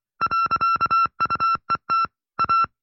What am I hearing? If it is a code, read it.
Morse code, heard as a keyed tone, about 25 wpm, 1350 Hz. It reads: .UET A